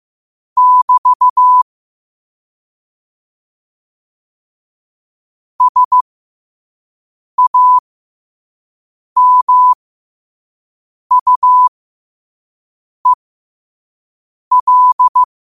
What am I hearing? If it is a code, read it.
Morse code, heard as a keyed tone, about 15 wpm, spaced as for 5 wpm, 1000 Hz. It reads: <BT> SAMUEL
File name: 5wpm